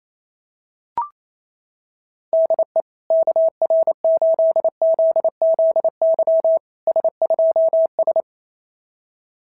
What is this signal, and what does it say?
Morse code, heard as a keyed tone, about 28 wpm, 650 Hz. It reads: DE KR8ZZY S2S
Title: SOTA QSO Practice 28wpm